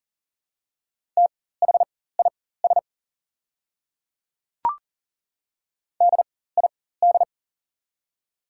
Morse code, heard as a keyed tone, about 40 wpm, spaced as for 19 wpm, 700 Hz. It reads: THIS DID
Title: 4x Character Spacing - Top 100 Words - 40wpm